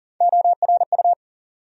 Morse code, heard as a keyed tone, about 40 wpm, 700 Hz. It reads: ORU
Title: USA Call Sign Postfixes - 3 Letters 40wpm